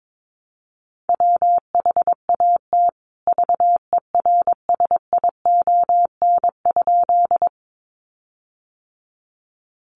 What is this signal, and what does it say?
Morse code, heard as a keyed tone, about 22 wpm, 700 Hz. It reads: WHAT VERSION?
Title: Sentences from Top 1400 Words 22wpm